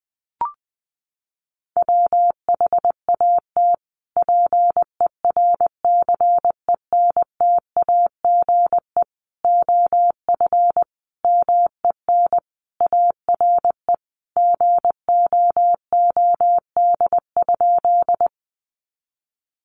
Morse code, heard as a keyed tone, about 19 wpm, 700 Hz. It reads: WHAT PERCENTAGE OF MEN ARE GOOD?
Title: Sentences from Top 1300 Words 20wpm